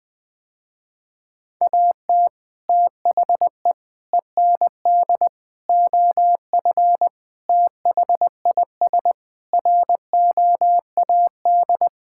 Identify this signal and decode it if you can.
Morse code, heard as a keyed tone, about 20 wpm, 700 Hz. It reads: AT THE END OF THIS ROAD
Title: Sentences with 5-Word Phrases - Spoken Beginning - 20wpm